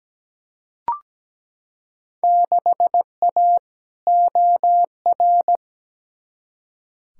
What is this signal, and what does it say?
Morse code, heard as a keyed tone, about 17 wpm, 700 Hz. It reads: 6A OR
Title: Field Day Exchanges - 17wpm